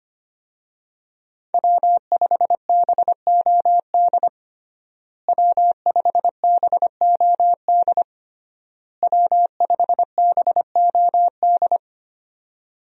Morse code, heard as a keyed tone, about 25 wpm, 700 Hz. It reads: W5BOD W5BOD W5BOD
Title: USA Call Signs - 5 Characters - Repeated 3x - 25wpm